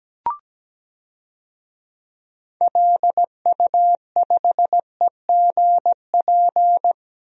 Morse code, heard as a keyed tone, about 17 wpm, 700 Hz. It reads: LU5EGP
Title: DXCC-100 Call Signs - 6 Characters - 17wpm